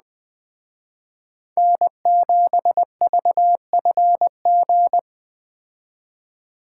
Morse code, heard as a keyed tone, about 20 wpm, 700 Hz. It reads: N7VFG
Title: Speed Racing - USA Call Signs 17wpm